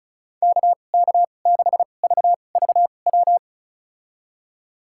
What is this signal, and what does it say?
Morse code, heard as a keyed tone, about 36 wpm, spaced as for 25 wpm, 700 Hz. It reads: KK6VVW
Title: USA Call Signs - 2x Character Spacing - 35wpm